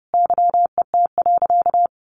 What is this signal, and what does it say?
Morse code, heard as a keyed tone, about 30 wpm, 700 Hz. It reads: YET.
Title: Easy Phrases 30wpm